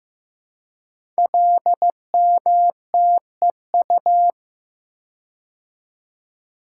Morse code, heard as a keyed tone, about 15 wpm, 700 Hz. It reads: LMTEU
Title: Code Groups - 5 Letters - 15wpm